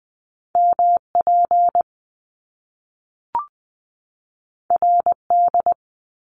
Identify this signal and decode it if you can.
Morse code, heard as a keyed tone, about 20 wpm, 700 Hz. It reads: MP RD
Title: Top Two Letter Combinations in Words 20wpm